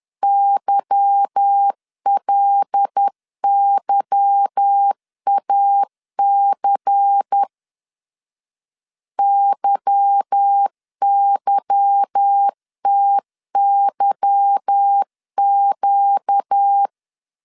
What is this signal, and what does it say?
Morse code, heard as a keyed tone, about 10 wpm, 800 Hz. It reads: YLYAC YYTYQ
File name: LEZIONI / 8 YJ45 / 8 Y 10wpm 2615